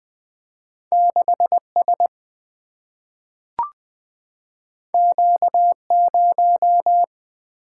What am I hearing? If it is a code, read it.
Morse code, heard as a keyed tone, about 20 wpm, 700 Hz. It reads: 6S Q0
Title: Single-Letter Single-Number 20wpm